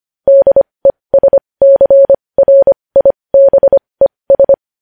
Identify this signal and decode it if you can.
Morse code, heard as a keyed tone, about 25 wpm, spaced as for 20 wpm, 550 Hz. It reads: DESCRIBES
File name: podcast / News headlines-1x-25x20